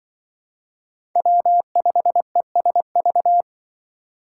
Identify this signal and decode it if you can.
Morse code, heard as a keyed tone, about 24 wpm, 700 Hz. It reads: W5ESV
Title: Speed Racing - USA Call Signs 20wpm